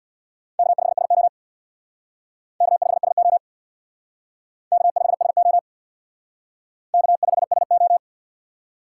Morse code, heard as a keyed tone, about 63 wpm, 700 Hz. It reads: K5SO K5SO K5SO K5SO